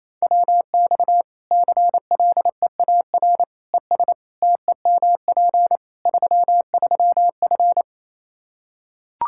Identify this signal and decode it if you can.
Morse code, heard as a keyed tone, about 28 wpm, 700 Hz. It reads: WX CLEAR ES TEMP 33F
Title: Practice QSO - Exchange 2 - Info - 28wpm